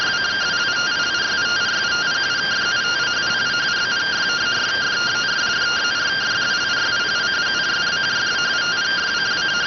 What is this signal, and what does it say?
Two-tone teleprinter signal, about 170 Hz shift.